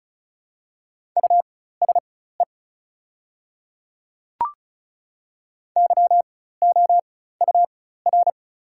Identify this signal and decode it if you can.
Morse code, heard as a keyed tone, about 34 wpm, spaced as for 16 wpm, 700 Hz. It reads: USE YOUR
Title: 4x Character Spacing - Top 100 Words - 35wpm